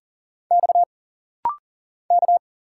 Morse code, heard as a keyed tone, about 39 wpm, 700 Hz. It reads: X K
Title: Single Letters - Rapid-Fire 40wpm